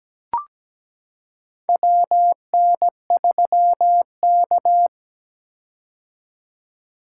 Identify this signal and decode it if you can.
Morse code, heard as a keyed tone, about 17 wpm, 700 Hz. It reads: WN3K
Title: K1USN SST QSO Practice - 17wpm